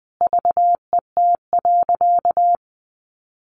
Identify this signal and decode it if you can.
Morse code, heard as a keyed tone, about 20 wpm, 700 Hz. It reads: VET.
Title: Easy Phrases 20wpm